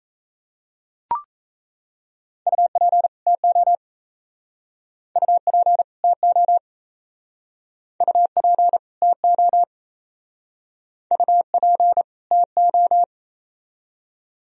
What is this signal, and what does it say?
Morse code, heard as a keyed tone, about 38 wpm, 700 Hz. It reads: UP TO UP TO UP TO UP TO